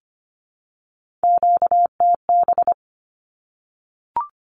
Morse code, heard as a keyed tone, about 25 wpm, 700 Hz. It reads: QTB
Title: Q-Codes 25wpm